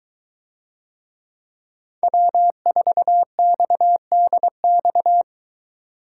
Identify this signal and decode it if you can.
Morse code, heard as a keyed tone, about 21 wpm, 700 Hz. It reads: W4XDX